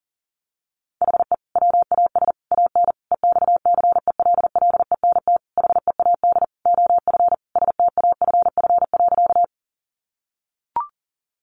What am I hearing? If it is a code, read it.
Morse code, heard as a keyed tone, about 40 wpm, 700 Hz. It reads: HE WAS AN EXCELLENT HEAD OF STAFF.
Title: Sentences from Top 1200 Words 40wpm